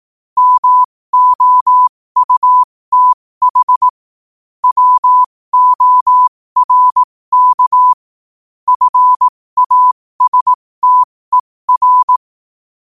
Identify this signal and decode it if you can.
Morse code, heard as a keyed tone, about 18 wpm, spaced as for 15 wpm, 1000 Hz. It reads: MOUTH WORK FASTER
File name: qotdfiles / 15wpm